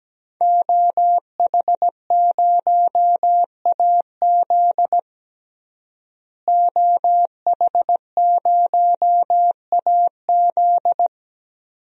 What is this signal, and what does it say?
Morse code, heard as a keyed tone, about 17 wpm, 700 Hz. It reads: OH0AZ OH0AZ
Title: DXCC-100 Call Signs - 5 Characters - Repeated 3x - 17wpm